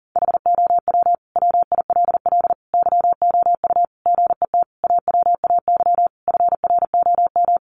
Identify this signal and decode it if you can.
Morse code, heard as a keyed tone, about 40 wpm, 700 Hz. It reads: HOW WILL YOU GET AWAY FROM
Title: Sentences from Top 1500 Words 40wpm